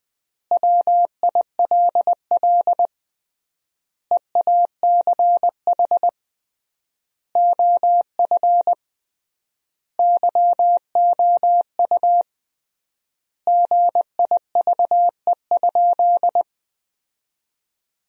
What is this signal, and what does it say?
Morse code, heard as a keyed tone, about 21 wpm, 700 Hz. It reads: WILL EACH OF YOU GIVE?
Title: 3x Word-Spacing - Sentences from Top 200 Words - 20wpm